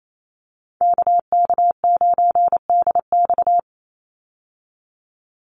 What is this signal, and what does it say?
Morse code, heard as a keyed tone, about 28 wpm, 700 Hz. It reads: KK9DX
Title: Call Signs Repeated 3x - 28wpm